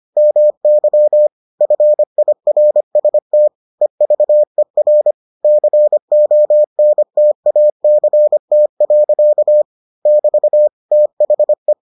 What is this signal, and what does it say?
Morse code, heard as a keyed tone, about 25 wpm, 600 Hz. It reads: MY FIRST EVER CONTACT. <BT> THE